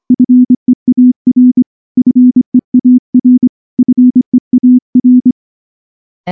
Morse code, heard as a keyed tone, about 24 wpm, 250 Hz. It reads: FEAR FEAR FEAR
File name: morse / morse-training-25WPM